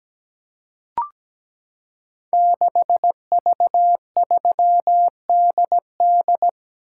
Morse code, heard as a keyed tone, about 17 wpm, 700 Hz. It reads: 6V3DD
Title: Code Groups - 5 Letter:Numbers - 17wpm